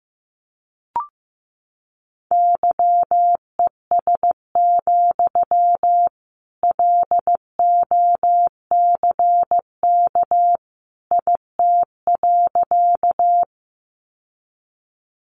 Morse code, heard as a keyed tone, about 15 wpm, 700 Hz. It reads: YES, LOCK IT.
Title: Sentences from Top 1300 Words 15wpm